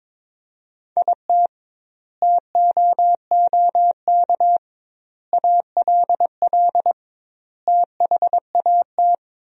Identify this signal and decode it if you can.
Morse code, heard as a keyed tone, about 22 wpm, 700 Hz. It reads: IT TOOK ALL THAT
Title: 2x Word-Spacing - Sentences from Top 300 Words - 22wpm